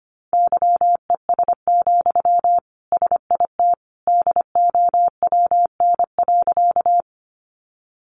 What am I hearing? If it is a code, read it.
Morse code, heard as a keyed tone, about 25 wpm, 700 Hz. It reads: YES, SIT DOWN.
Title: Sentences from Top 500 Words 25wpm